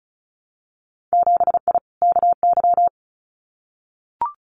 Morse code, heard as a keyed tone, about 36 wpm, 700 Hz. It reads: 7I KY
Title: Winter Field Day Exchanges Encore 35wpm